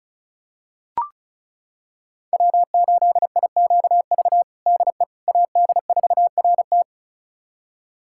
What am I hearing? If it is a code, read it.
Morse code, heard as a keyed tone, about 36 wpm, 700 Hz. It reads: W8IQV DE AD4RT
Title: USA Callsign DE Callsign 35wpm